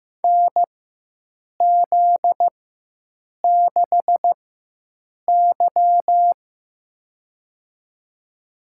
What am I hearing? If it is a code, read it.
Morse code, heard as a keyed tone, about 15 wpm, spaced as for 7 wpm, 700 Hz. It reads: NZ6Y
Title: USA Call Signs - 4 Characters - 4x Character Spacing - 15wpm